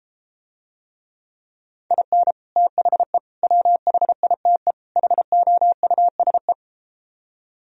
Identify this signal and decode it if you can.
Morse code, heard as a keyed tone, about 34 wpm, 700 Hz. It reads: IN THE WHITE HOUSE